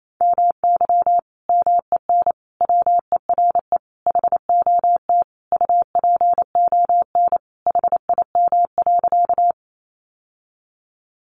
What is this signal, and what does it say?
Morse code, heard as a keyed tone, about 28 wpm, 700 Hz. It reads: MY MEN WERE HOT UPON HIM.
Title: Sentences from Top 1100 Words 28wpm